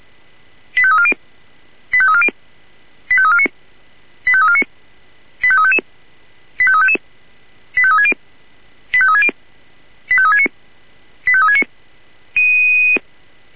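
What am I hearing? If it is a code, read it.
Five-tone selective call (ZVEI), heard as quick runs of five tones, tones of about 70 ms. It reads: 85340 85340 85349 85349 85355 85355 85360 85360 85369 85369
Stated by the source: Received on an SDR.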